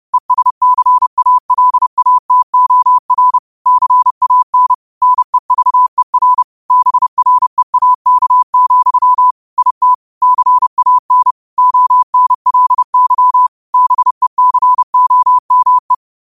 Morse code, heard as a keyed tone, about 30 wpm, 1000 Hz. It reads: ESCALATOR CAN NEVER BREAK, IT CAN ONLY BECOME
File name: qotdfiles / 30wpm